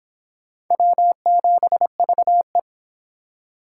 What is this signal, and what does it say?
Morse code, heard as a keyed tone, about 23 wpm, 700 Hz. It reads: W7VE